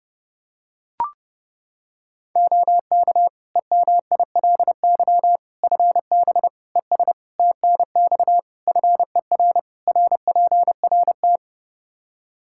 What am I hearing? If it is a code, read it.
Morse code, heard as a keyed tone, about 32 wpm, 700 Hz. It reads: OK EMILY FB ES TNX FER RPRT
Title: Practice QSO - Exchange 2 - Info - 30wpm